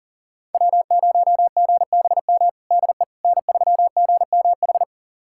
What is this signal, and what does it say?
Morse code, heard as a keyed tone, about 40 wpm, 700 Hz. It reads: W0GBM DE N3GMH
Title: USA Callsign DE Callsign - 5 Characters - 40wpm